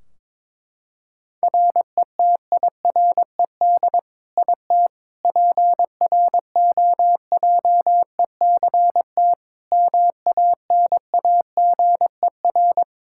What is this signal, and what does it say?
Morse code, heard as a keyed tone, about 22 wpm, 700 Hz. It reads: RETIRED IT PROJECT MANAGER
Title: Practice QSO - Exchange 3 - More Info - 22wpm